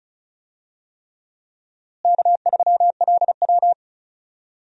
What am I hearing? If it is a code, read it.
Morse code, heard as a keyed tone, about 34 wpm, 700 Hz. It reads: K3LW
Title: USA Call Signs - 4 Characters - Repeated 3x - 35wpm